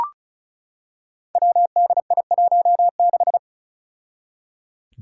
Morse code, heard as a keyed tone, about 35 wpm, 700 Hz. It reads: WDI16
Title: Five Challenging Letter-Number Combinations 35wpm